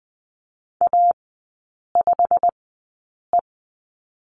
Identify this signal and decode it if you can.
Morse code, heard as a keyed tone, about 19 wpm, spaced as for 8 wpm, 700 Hz. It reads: A5E
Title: 100 Lesson 17e - Letter u - Three Characters 20wpm